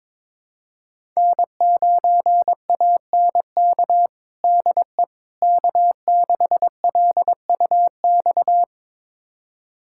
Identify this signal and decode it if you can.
Morse code, heard as a keyed tone, about 22 wpm, 700 Hz. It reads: N9ANK DE K6LUX
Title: USA Callsign DE Callsign 22wpm